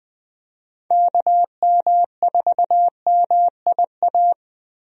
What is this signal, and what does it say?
Morse code, heard as a keyed tone, about 20 wpm, 700 Hz. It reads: KM4MIA
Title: USA Call Signs - 6 Characters - Repeated 3x - 20wpm